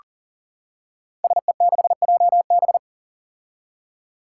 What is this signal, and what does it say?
Morse code, heard as a keyed tone, about 37 wpm, 700 Hz. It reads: SE6JB
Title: Five Challenging Letter-Number Combinations 40wpm